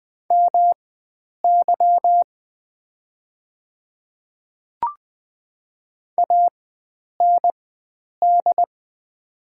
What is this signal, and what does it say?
Morse code, heard as a keyed tone, about 20 wpm, spaced as for 9 wpm, 700 Hz. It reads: MY AND
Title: 4x Character Spacing - Top 100 Words - 20wpm